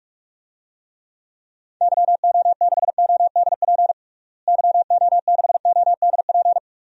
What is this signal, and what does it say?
Morse code, heard as a keyed tone, about 46 wpm, 700 Hz. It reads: YO6ODP YO6ODP
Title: DXCC-100 Call Signs - 6 Characters - Repeated 2x - 45wpm